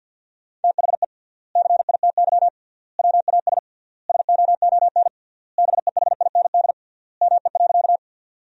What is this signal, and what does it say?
Morse code, heard as a keyed tone, about 50 wpm, 700 Hz. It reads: THE CITY WAS SOON BEHIND ME.
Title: 3x Word-Spacing - Sentences from Top 500 Words - 50wpm